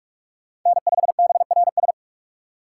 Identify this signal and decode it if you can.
Morse code, heard as a keyed tone, about 43 wpm, 700 Hz. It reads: N5BRS
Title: USA Call Signs - 5 Characters - 6x Character Spacing - 45wpm